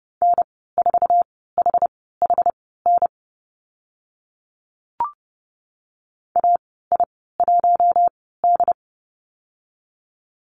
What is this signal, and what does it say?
Morse code, heard as a keyed tone, about 30 wpm, spaced as for 17 wpm, 700 Hz. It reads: N4HHN AI1D
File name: Lesson 14g - Number 2 - Call Signs 30wpm